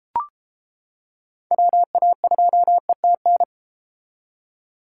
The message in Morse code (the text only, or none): WA2ETN